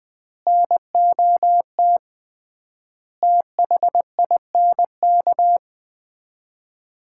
Morse code, heard as a keyed tone, about 21 wpm, 700 Hz. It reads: NOT THINK
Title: 3x Word-Spacing - Sets of 4 Words - 20wpm